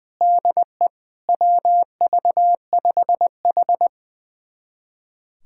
Morse code, heard as a keyed tone, about 20 wpm, 700 Hz. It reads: DE WV5H
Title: USA Callsign DE Callsign - 4 Characters - 20wpm